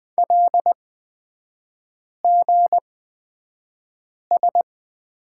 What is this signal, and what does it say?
Morse code, heard as a keyed tone, about 20 wpm, spaced as for 5 wpm, 700 Hz. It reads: LGS
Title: 243 Lesson 39e - Letter z - Three Characters 20wpm